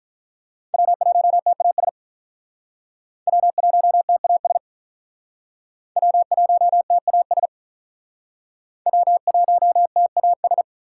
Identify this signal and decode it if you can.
Morse code, heard as a keyed tone, about 48 wpm, 700 Hz. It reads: W1TAS W1TAS W1TAS W1TAS